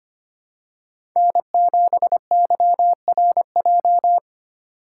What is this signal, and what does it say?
Morse code, heard as a keyed tone, about 25 wpm, 700 Hz. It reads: N7YRJ
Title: USA Call Signs - 3x Character Spacing - 25wpm